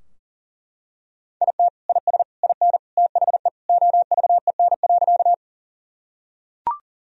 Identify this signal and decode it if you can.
Morse code, heard as a keyed tone, about 40 wpm, 700 Hz. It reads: IT IS IN THE OVEN.
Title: Sentences from Top 1000 Words - Review Part 4 - 40wpm